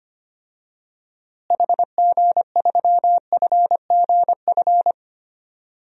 Morse code, heard as a keyed tone, about 25 wpm, 700 Hz. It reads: HG3FGF
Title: DXCC-100 Call Signs - 6 Characters - Repeated 3x - 25wpm